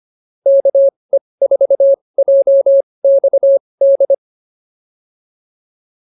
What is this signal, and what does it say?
Morse code, heard as a keyed tone, about 24 wpm, spaced as for 20 wpm, 550 Hz. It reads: KE4JXD